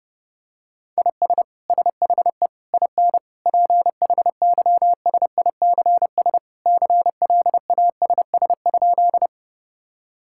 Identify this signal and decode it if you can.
Morse code, heard as a keyed tone, about 32 wpm, 700 Hz. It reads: IS SHE IN PHYSICS CLASS?
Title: Sentences from Top 1000 Words - Review Part 5 - 30wpm